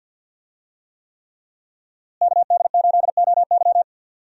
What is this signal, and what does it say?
Morse code, heard as a keyed tone, about 51 wpm, 700 Hz. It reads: KD8OY
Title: USA Call Signs - 5 Characters - Repeated 3x - 50wpm